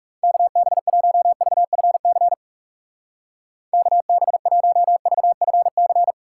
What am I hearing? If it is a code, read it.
Morse code, heard as a keyed tone, about 37 wpm, 700 Hz. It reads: KB1VFC KB1VFC
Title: Speed Racing - USA 2x3 Call Signs 30wpm